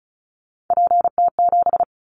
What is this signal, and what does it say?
Morse code, heard as a keyed tone, about 36 wpm, 700 Hz. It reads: PT7
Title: Callsign Prefixes for Top 50 DXCC - 35wpm